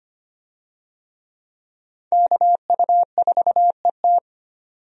Morse code, heard as a keyed tone, about 25 wpm, 700 Hz. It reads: KU4ET
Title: USA Call Signs - 5 Characters - 25wpm